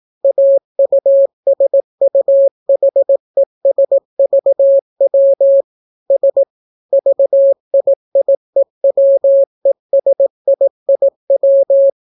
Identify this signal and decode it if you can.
Morse code, heard as a keyed tone, about 18 wpm, 550 Hz. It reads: AUSUHESVW S VIIEWESIIW